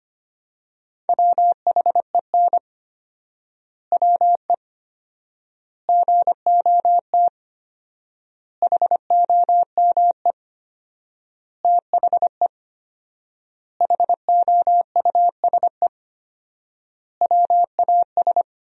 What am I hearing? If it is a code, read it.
Morse code, heard as a keyed tone, about 25 wpm, 700 Hz. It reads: WHEN WE GOT HOME THE HOUSE WAS
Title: 4x Word-Spacing - Sentences from Top 500 Words - 25wpm